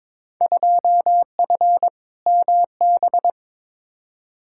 Morse code, heard as a keyed tone, about 22 wpm, 700 Hz. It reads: 2F MB
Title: Field Day Exchanges Encore - 22wpm